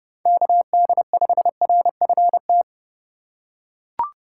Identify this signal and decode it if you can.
Morse code, heard as a keyed tone, about 29 wpm, 700 Hz. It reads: KD5RFT